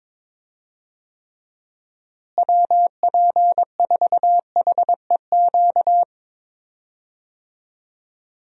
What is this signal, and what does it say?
Morse code, heard as a keyed tone, about 22 wpm, 700 Hz. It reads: WP4HEQ
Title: USA Call Signs - 6 Characters - Repeated 2x - 22wpm